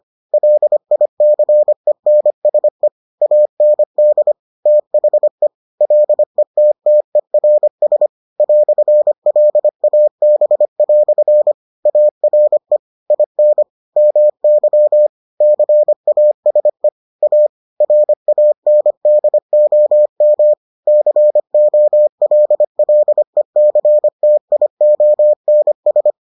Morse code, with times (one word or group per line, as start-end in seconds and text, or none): 0.3-2.9 LICENSE
3.2-4.3 AND
4.7-5.5 THE
5.8-8.1 LETTERS
8.4-11.5 "LAB"
11.9-12.8 ARE
13.1-13.6 IN
14.0-15.1 MY
15.4-16.9 CASE
17.2-17.5 A
17.8-20.5 RANDOM
20.9-26.1 COLLECTIONS